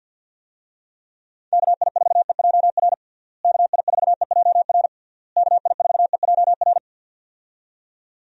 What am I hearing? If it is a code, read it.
Morse code, heard as a keyed tone, about 46 wpm, 700 Hz. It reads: KI4EJR KI4EJR KI4EJR
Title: USA Call Signs - 6 Characters - Repeated 3x - 50wpm